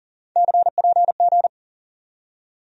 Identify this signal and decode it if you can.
Morse code, heard as a keyed tone, about 40 wpm, 700 Hz. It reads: CPG
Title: USA Call Sign Postfixes - 3 Letters 40wpm